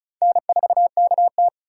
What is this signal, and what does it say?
Morse code, heard as a keyed tone, about 36 wpm, 700 Hz. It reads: N4KT
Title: USA Call Signs - 4 Characters - 2x Character Spacing - 35wpm